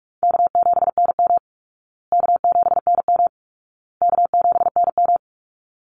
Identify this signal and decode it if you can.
Morse code, heard as a keyed tone, about 44 wpm, 700 Hz. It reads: K7NM K7NM K7NM
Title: Call Signs Repeated 2x - 45wpm